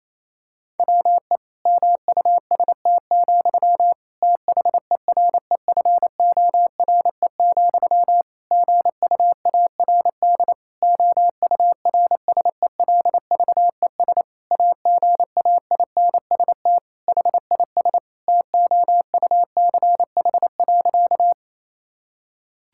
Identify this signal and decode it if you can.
Morse code, heard as a keyed tone, about 28 wpm, 700 Hz. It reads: WE MUST, THEREFORE, GUARD OURSELVES AGAINST HIS TOUCH.
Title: Sentences from Top 1600 Words 28wpm